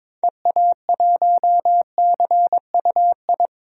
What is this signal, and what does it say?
Morse code, heard as a keyed tone, about 22 wpm, 700 Hz. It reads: EA1CUI
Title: DXCC-100 Call Signs - 6 Characters - 22wpm